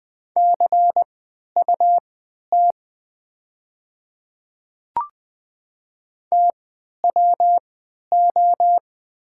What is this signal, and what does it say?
Morse code, heard as a keyed tone, about 20 wpm, spaced as for 11 wpm, 700 Hz. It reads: CUT TWO
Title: 3x Character Spacing - Top 300 Words - 20wpm